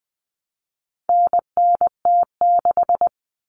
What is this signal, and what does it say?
Morse code, heard as a keyed tone, about 20 wpm, 700 Hz. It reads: NNT6
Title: Common Cut Numbers - 4-Digit 20wpm